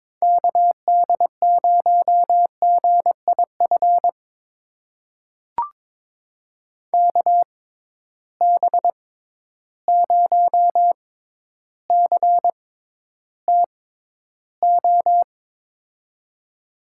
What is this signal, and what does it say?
Morse code, heard as a keyed tone, about 22 wpm, spaced as for 11 wpm, 700 Hz. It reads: KD0GIF KB0CTO
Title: USA Call Signs - 6 Characters - 6x Character Spacing - 22wpm